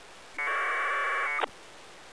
Packet radio (AX.25, as in APRS).